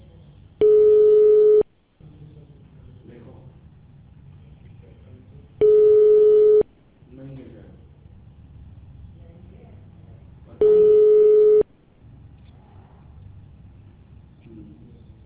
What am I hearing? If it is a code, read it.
Ringback tone (Europe, 425 Hz), with speech.